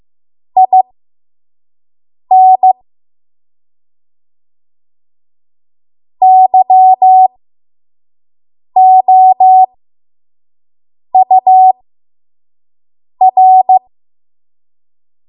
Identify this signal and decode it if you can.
Morse code, heard as a keyed tone, about 15 wpm, spaced as for 5 wpm, 750 Hz. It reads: IN YOUR
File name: Morse / 05WPM